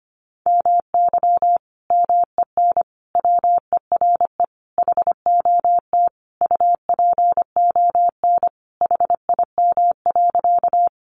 Morse code, heard as a keyed tone, about 25 wpm, 700 Hz. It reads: MY MEN WERE HOT UPON HIM.
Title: Sentences from Top 1100 Words 25wpm